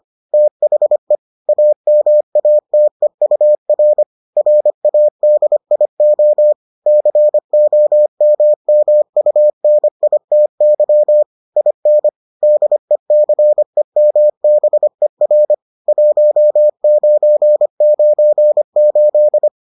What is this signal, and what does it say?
Morse code, heard as a keyed tone, about 25 wpm, 600 Hz. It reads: THE AMATEUR RADIO COMMUNITY IN DECEMBER 1998